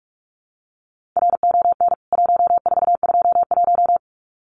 Morse code, heard as a keyed tone, about 44 wpm, 700 Hz. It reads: RON 1421
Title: CWT Exchange 45wpm